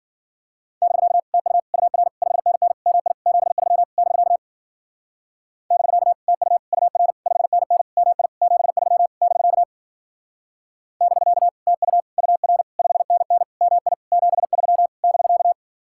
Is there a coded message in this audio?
Yes: Morse code from 0.8 to 15.5 s, heard as a keyed tone, about 51 wpm, 700 Hz. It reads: <BK> TU UR 5NN MI 73 <BK> <BK> TU UR 5NN MI 73 <BK> <BK> TU UR 5NN MI 73 <BK>